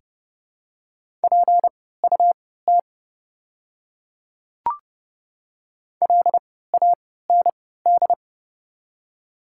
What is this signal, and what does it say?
Morse code, heard as a keyed tone, about 30 wpm, spaced as for 17 wpm, 700 Hz. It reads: PUT LAND
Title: 3x Character Spacing - Top 300 Words - 30wpm